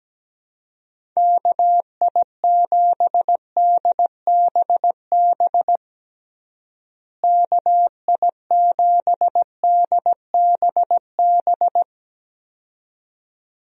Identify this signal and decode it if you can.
Morse code, heard as a keyed tone, about 17 wpm, 700 Hz. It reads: KI7DBB KI7DBB
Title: USA Call Signs - 6 Characters - Repeated 2x - 17wpm